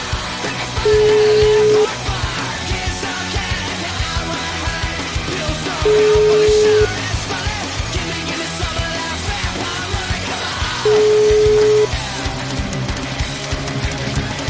Ringback tone (Europe, 425 Hz), with noise.